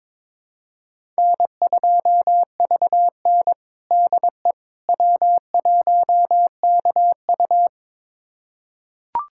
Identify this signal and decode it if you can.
Morse code, heard as a keyed tone, about 22 wpm, 700 Hz. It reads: N2VN DE W1KU
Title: USA Callsign DE Callsign 22wpm